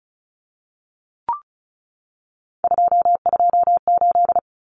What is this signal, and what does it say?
Morse code, heard as a keyed tone, about 34 wpm, 700 Hz. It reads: 228
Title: Numbers - 3-Digit 35wpm